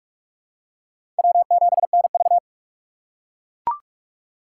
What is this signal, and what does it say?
Morse code, heard as a keyed tone, about 41 wpm, 700 Hz. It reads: W7NV
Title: USA Call Signs - 4 Characters - 3x Character Spacing - 45wpm